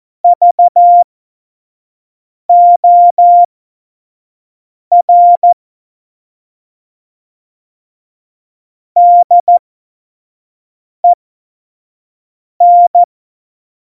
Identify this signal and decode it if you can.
Morse code, heard as a keyed tone, about 14 wpm, spaced as for 5 wpm, 700 Hz. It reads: VOR DEN